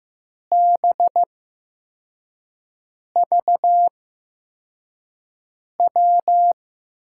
Morse code, heard as a keyed tone, about 15 wpm, spaced as for 4 wpm, 700 Hz. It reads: BVW
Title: USA Call Signs - 6 Characters - 8x Character Spacing - 15wpm